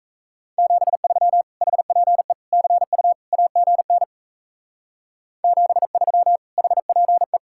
Morse code, heard as a keyed tone, about 34 wpm, 700 Hz. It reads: 73 HPE CU AGN 73 HPE
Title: Speed-Racing Common QSO Phrases - 28wpm